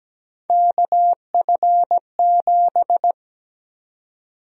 Morse code, heard as a keyed tone, about 17 wpm, 700 Hz. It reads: KF7
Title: USA Call Sign Prefixes - 17wpm